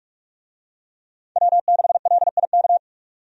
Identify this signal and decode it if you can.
Morse code, heard as a keyed tone, about 43 wpm, 700 Hz. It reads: W6LIK
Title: USA Call Signs - 5 Characters - 45wpm